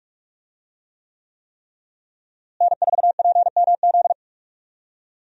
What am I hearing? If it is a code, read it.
Morse code, heard as a keyed tone, about 41 wpm, 700 Hz. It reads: N4PMZ
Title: USA Call Signs - 5 Characters - Repeated 3x - 45wpm